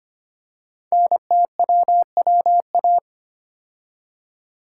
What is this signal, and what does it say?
Morse code, heard as a keyed tone, about 25 wpm, 700 Hz. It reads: NTWWA
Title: Code Groups - 5 Letters - 25wpm